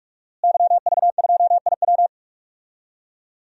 Morse code, heard as a keyed tone, about 46 wpm, 700 Hz. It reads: YV2IW
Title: DXCC-100 Call Signs - 5 Characters - Repeated 2x - 45wpm